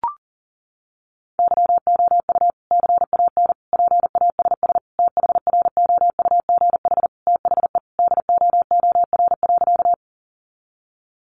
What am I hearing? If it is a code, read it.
Morse code, heard as a keyed tone, about 40 wpm, 700 Hz. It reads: YOU CAN PASS THROUGH THE DOOR.
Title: Sentences from Top 400 Words 40wpm